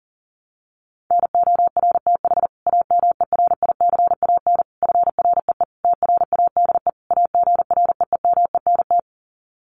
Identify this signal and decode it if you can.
Morse code, heard as a keyed tone, about 40 wpm, 700 Hz. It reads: NORTH AMERICAN FREE TRADE AGREEMENT
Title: Sets of 5 Words Encore 40wpm